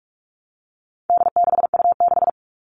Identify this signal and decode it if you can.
Morse code, heard as a keyed tone, about 44 wpm, 700 Hz. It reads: D6U6
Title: All Cut Numbers - 4-Digit 45wpm